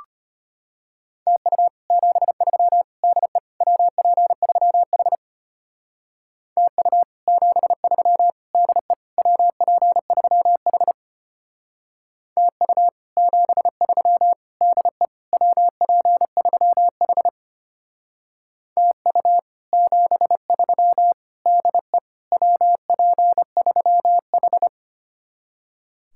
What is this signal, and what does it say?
Morse code, heard as a keyed tone, about 33 wpm, 700 Hz. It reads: TU 73 DE WP3H TU 73 DE WP3H TU 73 DE WP3H TU 73 DE WP3H